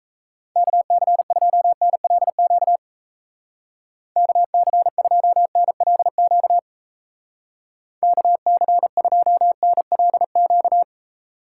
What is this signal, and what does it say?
Morse code, heard as a keyed tone, about 34 wpm, 700 Hz. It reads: KC2NLQ KC2NLQ KC2NLQ
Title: Speed Racing - USA 2x3 Call Signs 28wpm